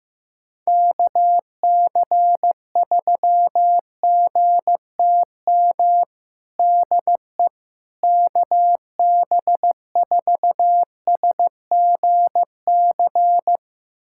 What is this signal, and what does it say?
Morse code, heard as a keyed tone, about 16 wpm, 700 Hz. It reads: KC3GTM DE KB4SGC
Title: USA Callsign DE Callsign - 6 Characters - 15wpm